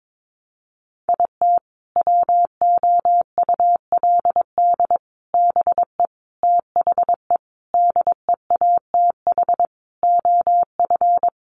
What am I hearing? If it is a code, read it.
Morse code, heard as a keyed tone, about 22 wpm, 700 Hz. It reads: IT WOULD BE THE DEATH OF
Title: Sentences from Top 1400 Words 22wpm